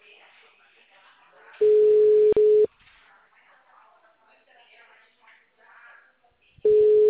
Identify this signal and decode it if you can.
Ringback tone (Europe, 425 Hz), with speech.
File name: out- -2055- - - .44585